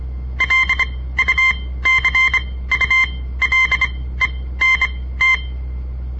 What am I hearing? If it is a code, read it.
Morse code, heard as a keyed tone, about 24 wpm, spaced as for 16 wpm, 2100 Hz. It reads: LUCULENT
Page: On the air Morse code practice from the Southern Montana Amatuer Radio Association; all code, no read back